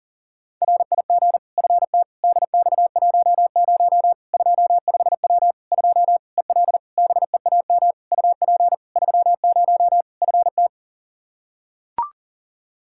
Morse code, heard as a keyed tone, about 40 wpm, 700 Hz. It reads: RIG FT DX10 25W 2 EL BEAM UP 30 FT
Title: Practice QSO - Exchange 2 - Info - 40wpm